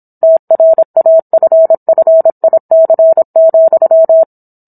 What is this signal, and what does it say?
Morse code, heard as a keyed tone, about 27 wpm, 650 Hz. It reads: TRAFFIC,